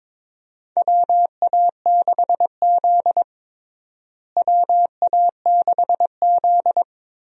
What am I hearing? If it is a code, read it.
Morse code, heard as a keyed tone, about 22 wpm, 700 Hz. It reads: WA6Z WA6Z
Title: USA Call Signs - 4 Characters - Repeated 3x - 22wpm